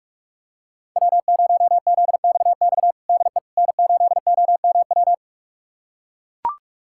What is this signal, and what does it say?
Morse code, heard as a keyed tone, about 41 wpm, 700 Hz. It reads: W0ZXX DE N8OMW
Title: USA Callsign DE Callsign - 5 Characters - 45wpm